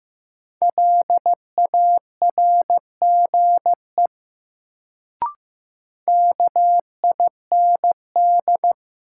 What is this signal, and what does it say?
Morse code, heard as a keyed tone, about 15 wpm, 700 Hz. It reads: LARGE KIND
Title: Sentences from Top 200 Words - New Word Drill - 15wpm